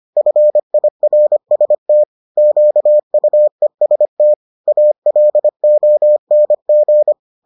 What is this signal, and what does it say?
Morse code, heard as a keyed tone, about 25 wpm, 600 Hz. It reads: FIRST QUEST ALONG